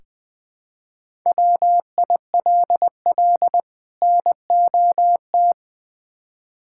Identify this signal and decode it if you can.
Morse code, heard as a keyed tone, about 20 wpm, 700 Hz. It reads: WILL NOT
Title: Build A Phrase - 5-Word Phrases - 20wpm